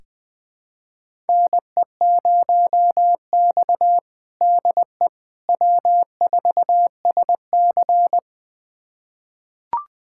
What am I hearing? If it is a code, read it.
Morse code, heard as a keyed tone, about 21 wpm, 700 Hz. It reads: NE0X DE W4SC
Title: USA Callsign DE Callsign - 4 Characters - 20wpm